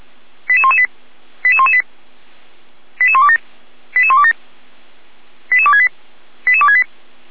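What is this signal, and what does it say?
Five-tone selective call (ZVEI), heard as quick runs of five tones, tones of about 70 ms. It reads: 88118 88118 88127 88127 88268 88268
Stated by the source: ZVEI five-tone selective call received with an SDR, Feuerwehr Orlamünde.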